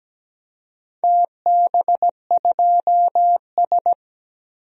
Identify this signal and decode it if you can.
Morse code, heard as a keyed tone, about 17 wpm, 700 Hz. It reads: TB2S
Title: DXCC-100 Call Signs - 4 Characters - Repeated 3x - 17wpm